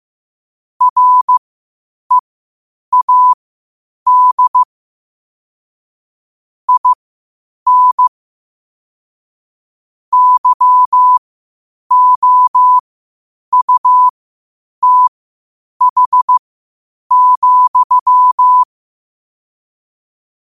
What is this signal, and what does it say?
Morse code, heard as a keyed tone, about 15 wpm, spaced as for 8 wpm, 1000 Hz. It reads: READ IN YOUTH,
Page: qotd-at-8-wpm